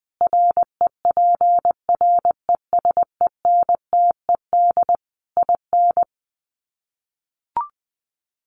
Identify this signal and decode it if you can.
Morse code, heard as a keyed tone, about 20 wpm, 700 Hz. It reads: REPRESENTED IN
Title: Sets of 5 Words 20wpm